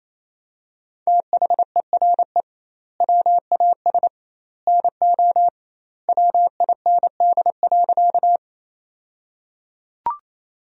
Morse code, heard as a keyed tone, about 27 wpm, 700 Hz. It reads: THERE WAS NO WIND.
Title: 2x Word-Spacing - Sentences from Top 400 Words - 28wpm